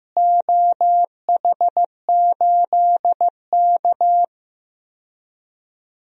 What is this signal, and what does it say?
Morse code, heard as a keyed tone, about 15 wpm, 700 Hz. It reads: OH8K
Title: DXCC-100 Call Signs - 4 Characters - Repeated 3x - 15wpm